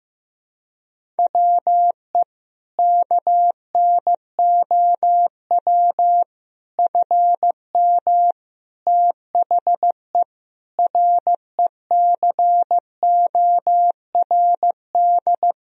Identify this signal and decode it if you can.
Morse code, heard as a keyed tone, about 15 wpm, 700 Hz. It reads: WE KNOW FM THE RECORD
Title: Sentences from Top 500 Words and Abbreviations 15wpm